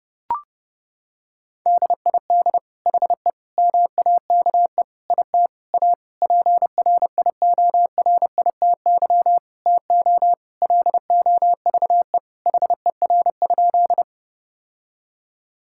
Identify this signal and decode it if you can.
Morse code, heard as a keyed tone, about 30 wpm, 700 Hz. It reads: DID HE MAKE IT A PRIORITY TO LOVE HER?
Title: Sentences from Top 1600 Words 30wpm